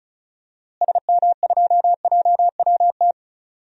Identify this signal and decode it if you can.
Morse code, heard as a keyed tone, about 36 wpm, 700 Hz. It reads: SM2JWT
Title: DXCC-100 Call Signs - 6 Characters - Repeated 2x - 35wpm